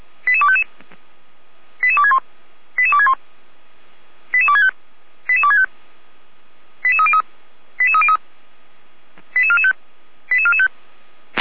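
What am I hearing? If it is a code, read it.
Five-tone selective call (ZVEI), heard as quick runs of five tones, tones of about 70 ms. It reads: 88255 88261 88261 88265 88265 88303 88303 88445 88445
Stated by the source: ZVEI five-tone selective call 88247 received with an SDR, from the Feuerwehr Orlamünde audio server.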